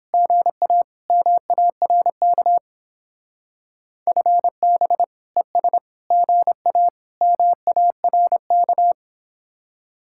Morse code, heard as a keyed tone, about 29 wpm, 700 Hz. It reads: GA MARK FB ES GA MARK